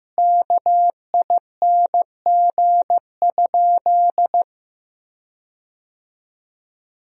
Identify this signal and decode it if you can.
Morse code, heard as a keyed tone, about 15 wpm, 700 Hz. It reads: KING?
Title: 2x Word-Spacing - Sentences from Top 400 Words - 15wpm